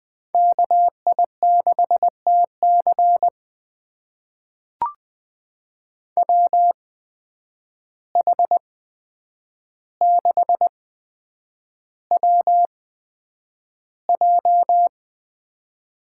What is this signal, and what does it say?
Morse code, heard as a keyed tone, about 20 wpm, spaced as for 9 wpm, 700 Hz. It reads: KI6TC WH6WJ
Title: USA Call Signs - 5 Characters - 8x Character Spacing - 20wpm